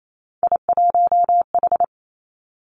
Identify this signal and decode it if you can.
Morse code, heard as a keyed tone, about 28 wpm, 700 Hz. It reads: I1H
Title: Three Challenging Letter-Number Combinations 28wpm